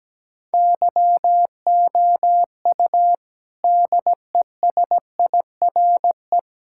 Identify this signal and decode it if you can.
Morse code, heard as a keyed tone, about 17 wpm, 700 Hz. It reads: YOU DESIRE
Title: Sentences from Top 1600 Words 17wpm